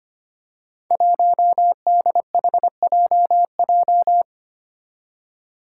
Morse code, heard as a keyed tone, about 25 wpm, 700 Hz. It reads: 1DHJJ
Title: Five Challenging Letter-Number Combinations 25wpm